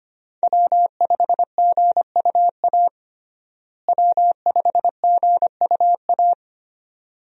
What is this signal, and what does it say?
Morse code, heard as a keyed tone, about 25 wpm, 700 Hz. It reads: W5GUA W5GUA
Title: USA Call Signs - 5 Characters - Repeated 2x - 25wpm